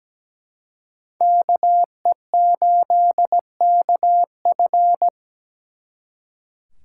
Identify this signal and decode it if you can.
Morse code, heard as a keyed tone, about 20 wpm, 700 Hz. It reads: KE8KF
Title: Speed Racing - USA 2x2 Call Signs 17wpm